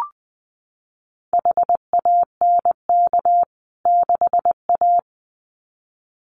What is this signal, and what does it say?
Morse code, heard as a keyed tone, about 20 wpm, 700 Hz. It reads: HANK 6A
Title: CWT Exchange 20wpm